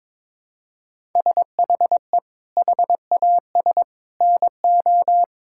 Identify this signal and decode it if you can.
Morse code, heard as a keyed tone, about 22 wpm, 700 Hz. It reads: SHE HAS NO
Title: Sentences from Top 1000 Words - Review Part 6 - 22wpm